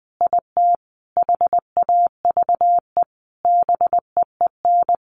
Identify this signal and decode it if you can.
Morse code, heard as a keyed tone, about 20 wpm, 700 Hz. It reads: IT HAVE BEEN
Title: Sentences from Top 1200 Words 20wpm